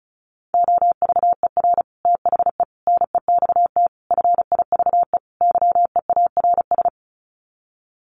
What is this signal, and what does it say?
Morse code, heard as a keyed tone, about 34 wpm, 700 Hz. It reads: OVER THE NEXT FIVE YEARS
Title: Sets of 5 Words 35wpm